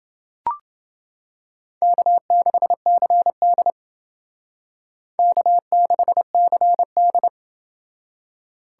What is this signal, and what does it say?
Morse code, heard as a keyed tone, about 30 wpm, 700 Hz. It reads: K6CD K6CD